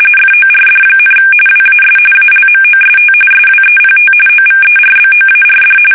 RTTY, about 45.45 baud. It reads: FORD, WAS ALMOST SWEPT OVERBOARD
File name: fsk850 / battleshipnj